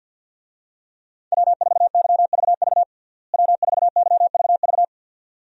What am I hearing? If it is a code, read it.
Morse code, heard as a keyed tone, about 52 wpm, 700 Hz. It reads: W4YVV W4YVV
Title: USA Call Signs - 5 Characters - Repeated 2x - 50wpm